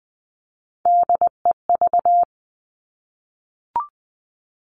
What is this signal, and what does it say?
Morse code, heard as a keyed tone, about 19 wpm, 700 Hz. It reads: DEV
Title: Three Challenging Letter-Number Combinations 20wpm